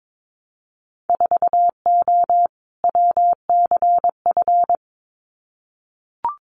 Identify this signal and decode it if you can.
Morse code, heard as a keyed tone, about 22 wpm, 700 Hz. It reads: 4O WCF
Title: Winter Field Day Exchanges Encore 22wpm